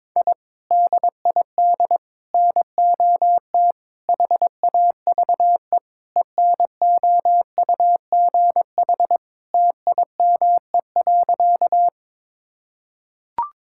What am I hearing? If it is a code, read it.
Morse code, heard as a keyed tone, about 22 wpm, 700 Hz. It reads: I DID NOT HAVE ENOUGH TIME.
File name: practice / 22wpm / Sentences with 4-Word Phrases - 22wpm